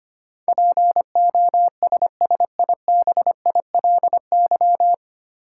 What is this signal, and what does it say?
Morse code, heard as a keyed tone, about 25 wpm, 700 Hz. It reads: POSSIBILY
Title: Sentences from Top 1300 Words - New Word Drill Alt - 25wpm